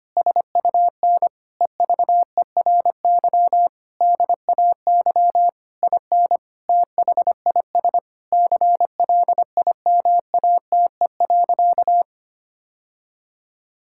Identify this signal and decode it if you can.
Morse code, heard as a keyed tone, about 25 wpm, 700 Hz. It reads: SUN EVERY DAY IN THIS CLIMATE.
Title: Sentences from Top 1600 Words 25wpm